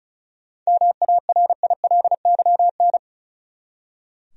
Morse code, heard as a keyed tone, about 36 wpm, 700 Hz. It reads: MARILYN
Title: USA Common Names - 35wpm